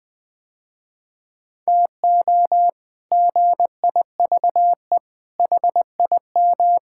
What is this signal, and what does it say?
Morse code, heard as a keyed tone, about 20 wpm, 700 Hz. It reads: TO GIVE HIM
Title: Sentences with 8-Word Phrases - Spoken Beginning - 20wpm